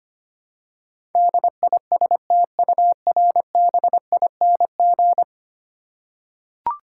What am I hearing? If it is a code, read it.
Morse code, heard as a keyed tone, about 25 wpm, 700 Hz. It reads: DISTURBING
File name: 10 Letter Words - First 3 Letters Given - 25wpm